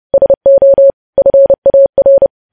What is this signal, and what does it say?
Morse code, heard as a keyed tone, about 30 wpm, 550 Hz. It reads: SO FAR
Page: news headlines in morse code at 30 wpm